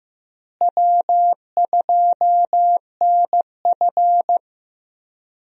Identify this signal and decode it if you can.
Morse code, heard as a keyed tone, about 15 wpm, 700 Hz. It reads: W2NF
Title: USA Call Signs - 4 Characters - Repeated 3x - 15wpm